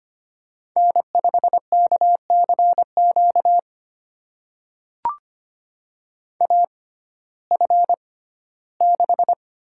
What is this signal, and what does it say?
Morse code, heard as a keyed tone, about 25 wpm, spaced as for 15 wpm, 700 Hz. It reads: N5KCQ AF6
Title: USA Call Signs - 5 Characters - 6x Character Spacing - 25wpm